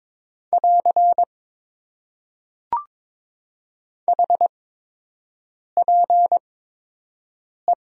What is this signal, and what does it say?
Morse code, heard as a keyed tone, about 22 wpm, spaced as for 6 wpm, 700 Hz. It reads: <AR> HPE
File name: Top 90 QSO Elements - 8x Character Spacing - 22wpm